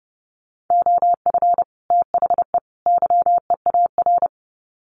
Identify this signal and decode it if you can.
Morse code, heard as a keyed tone, about 30 wpm, 700 Hz. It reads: OF THE YEAR
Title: Sets of 3 Words 30wpm